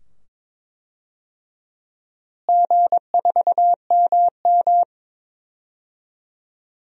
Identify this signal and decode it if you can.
Morse code, heard as a keyed tone, about 23 wpm, 700 Hz. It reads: G4MM
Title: DXCC-100 Call Signs - 4 Characters - Repeated 2x - 22wpm